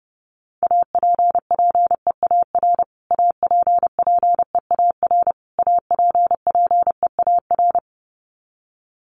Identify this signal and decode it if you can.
Morse code, heard as a keyed tone, about 28 wpm, 700 Hz. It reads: APPEAR APPEAR APPEAR
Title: Top 500 Words Repeated 30wpm